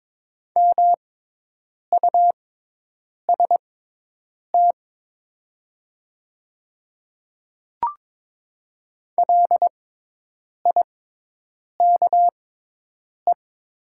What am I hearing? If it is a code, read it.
Morse code, heard as a keyed tone, about 22 wpm, spaced as for 8 wpm, 700 Hz. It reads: MUST LIKE
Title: 6x Character Spacing - Top 200 Words - 22wpm